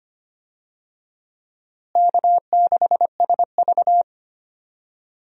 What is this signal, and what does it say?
Morse code, heard as a keyed tone, about 25 wpm, 700 Hz. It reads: K6SV
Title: USA Call Signs - 4 Characters - Repeated 2x - 25wpm